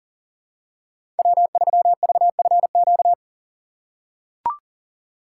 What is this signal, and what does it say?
Morse code, heard as a keyed tone, about 40 wpm, 700 Hz. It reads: W3VFQ
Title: USA Call Signs - 5 Characters - 6x Character Spacing - 40wpm